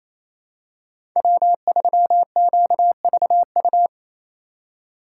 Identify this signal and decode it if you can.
Morse code, heard as a keyed tone, about 28 wpm, 700 Hz. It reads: W3QVU
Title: USA Call Signs - 5 Characters - 2x Character Spacing - 28wpm